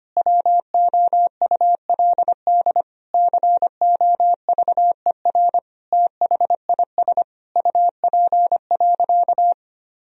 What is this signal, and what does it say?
Morse code, heard as a keyed tone, about 25 wpm, 700 Hz. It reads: WOULD COVER THIS UP.
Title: Sentences from Top 1000 Words - Review Part 1 - 25wpm